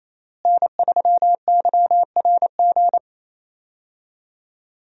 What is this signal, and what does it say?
Morse code, heard as a keyed tone, about 28 wpm, 700 Hz. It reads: N3YRG
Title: USA Call Signs - 5 Characters - Repeated 3x - 28wpm